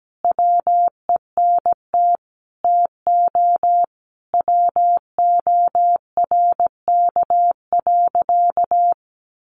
Morse code, heard as a keyed tone, about 17 wpm, 700 Hz. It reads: WENT TO WORK.
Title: Sentences from Top 1300 Words 17wpm